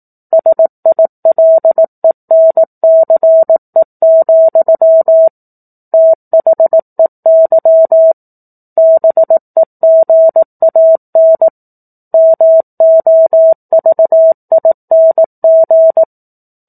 Morse code, heard as a keyed tone, about 18 wpm, 650 Hz. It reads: SILENCE, THEY BEGAN MOVING
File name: 04-Chapter-04 18 WPM